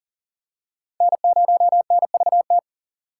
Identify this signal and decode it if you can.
Morse code, heard as a keyed tone, about 40 wpm, 700 Hz. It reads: N0NVT
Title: USA Call Signs - 5 Characters - 4x Character Spacing - 40wpm